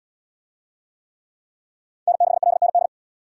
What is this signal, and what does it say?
Morse code, heard as a keyed tone, about 72 wpm, 700 Hz. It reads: N6LND